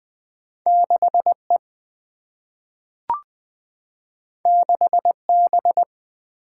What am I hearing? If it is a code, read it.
Morse code, heard as a keyed tone, about 20 wpm, 700 Hz. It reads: 6 E 6 B
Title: Single-Letter Single-Number 20wpm